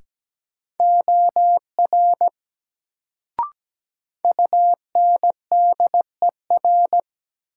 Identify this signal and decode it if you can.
Morse code, heard as a keyed tone, about 17 wpm, 700 Hz. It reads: OR UNDER
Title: Sets of 2 Words - New Word Drill - 17wpm